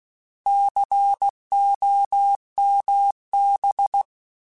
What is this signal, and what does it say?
Morse code, heard as a keyed tone, about 16 wpm, 800 Hz. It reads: COMB
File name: Alphabetical / Alphabetical mp3s / Cs @15